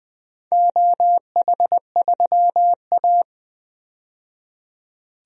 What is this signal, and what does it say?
Morse code, heard as a keyed tone, about 20 wpm, 700 Hz. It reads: OH3A
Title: DXCC-100 Call Signs - 4 Characters - 20wpm